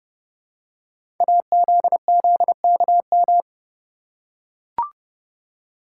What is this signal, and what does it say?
Morse code, heard as a keyed tone, about 30 wpm, 700 Hz. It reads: AZZKM
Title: Code Groups - 5 Letters - 30wpm